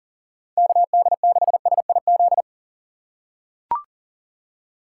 Morse code, heard as a keyed tone, about 40 wpm, 700 Hz. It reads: KD6SIZ
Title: USA Call Signs - 4x Character Spacing - 40wpm